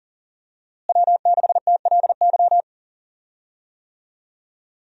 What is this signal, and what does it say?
Morse code, heard as a keyed tone, about 40 wpm, 700 Hz. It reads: W6TLY
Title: USA Call Signs - 5 Characters - Repeated 3x - 40wpm